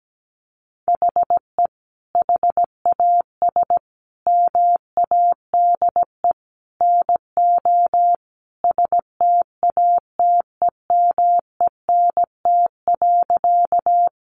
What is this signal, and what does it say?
Morse code, heard as a keyed tone, about 17 wpm, 700 Hz. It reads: HE HAS MADE NO STATEMENT.
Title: Sentences from Top 1200 Words 17wpm